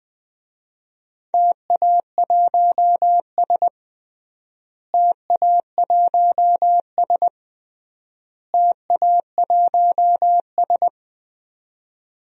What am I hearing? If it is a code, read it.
Morse code, heard as a keyed tone, about 20 wpm, 700 Hz. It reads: TA1S TA1S TA1S
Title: DXCC-100 Call Signs - 4 Characters - Repeated 3x - 20wpm